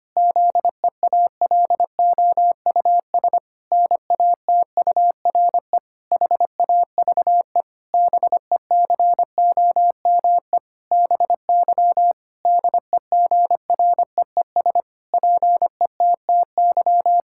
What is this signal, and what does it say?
Morse code, heard as a keyed tone, about 25 wpm, 700 Hz. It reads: ZEALOUS NATURE HAVE BECOME BY DEGREES PETTY
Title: Pangram Challenge v2 - 25wpm